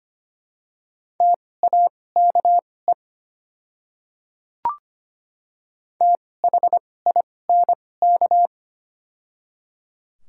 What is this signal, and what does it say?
Morse code, heard as a keyed tone, about 25 wpm, spaced as for 18 wpm, 700 Hz. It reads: TAKE THINK
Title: 2x Character Spacing - Top 200 Words - 25wpm